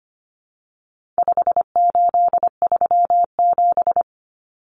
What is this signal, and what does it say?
Morse code, heard as a keyed tone, about 25 wpm, 700 Hz. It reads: 5837
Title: Numbers - 4-Digit 25wpm